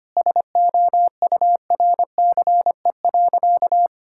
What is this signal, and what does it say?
Morse code, heard as a keyed tone, about 25 wpm, 700 Hz. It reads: SOURCE.
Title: Sentences from Top 1000 Words - Review Part 5 - 25wpm